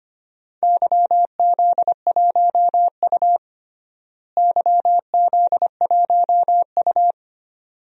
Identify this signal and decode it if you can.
Morse code, heard as a keyed tone, about 25 wpm, 700 Hz. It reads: YZ1U YZ1U
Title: DXCC-100 Call Signs - 4 Characters - Repeated 2x - 25wpm